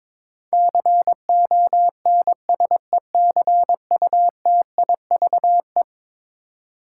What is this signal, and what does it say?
Morse code, heard as a keyed tone, about 22 wpm, 700 Hz. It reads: CONSECUTIVE
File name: practice / 22wpm / Eleven Letter Words - 22wpm